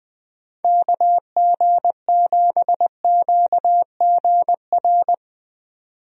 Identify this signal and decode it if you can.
Morse code, heard as a keyed tone, about 20 wpm, 700 Hz. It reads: KG7QGR
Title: USA Call Signs - 6 Characters - 20wpm